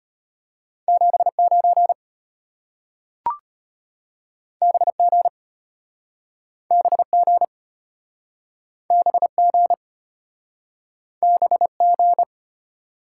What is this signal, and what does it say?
Morse code, heard as a keyed tone, about 33 wpm, 700 Hz. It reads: 79 BG BG BG BG